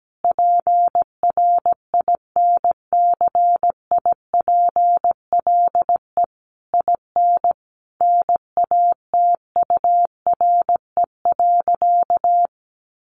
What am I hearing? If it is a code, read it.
Morse code, heard as a keyed tone, about 17 wpm, 700 Hz. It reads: PRINCIPLE IN NATURE.
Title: Sentences from Top 1400 Words 17wpm